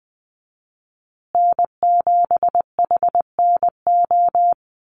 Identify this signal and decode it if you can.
Morse code, heard as a keyed tone, about 20 wpm, 700 Hz. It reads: N7HNO
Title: USA Call Signs Repeated 2x 20wpm